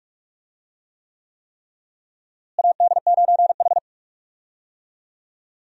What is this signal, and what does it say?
Morse code, heard as a keyed tone, about 46 wpm, 700 Hz. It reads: AD9H
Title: USA Call Signs - 4 Characters - Repeated 3x - 45wpm